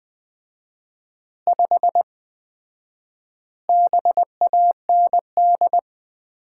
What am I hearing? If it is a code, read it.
Morse code, heard as a keyed tone, about 20 wpm, 700 Hz. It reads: 5 BAND
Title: Common QSO Phrases - 4x Word-Spacing - 20wpm